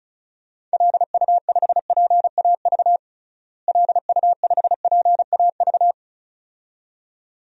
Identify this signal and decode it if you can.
Morse code, heard as a keyed tone, about 36 wpm, 700 Hz. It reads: LU5PAV LU5PAV
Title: DXCC-100 Call Signs - 6 Characters - Repeated 2x - 35wpm